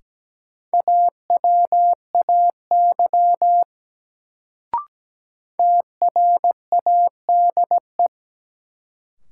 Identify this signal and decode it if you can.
Morse code, heard as a keyed tone, about 17 wpm, 700 Hz. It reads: AWAY TRADE
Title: Sentences from Top 600 Words - New Word Drill - 17wpm